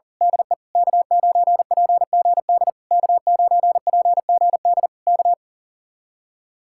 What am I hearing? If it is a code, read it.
Morse code, heard as a keyed tone, about 38 wpm, 700 Hz. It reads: DE K9PGD K9PGD K
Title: Speed-Racing Common QSO Phrases Encore - 40wpm